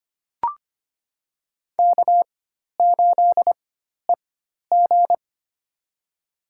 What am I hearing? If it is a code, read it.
Morse code, heard as a keyed tone, about 25 wpm, spaced as for 12 wpm, 700 Hz. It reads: K8EG
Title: USA Call Signs - 4 Characters - 4x Character Spacing - 25wpm